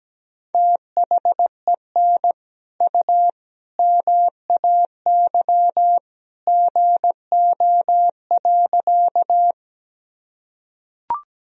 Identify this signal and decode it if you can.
Morse code, heard as a keyed tone, about 17 wpm, 700 Hz. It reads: THEN U MAY GO.
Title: Sentences from Top 100 Words and Abbreviations 17wpm